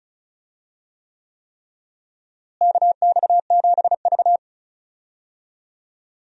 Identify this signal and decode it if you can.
Morse code, heard as a keyed tone, about 36 wpm, 700 Hz. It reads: KX7V
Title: USA Call Signs - 4 Characters - Repeated 3x - 35wpm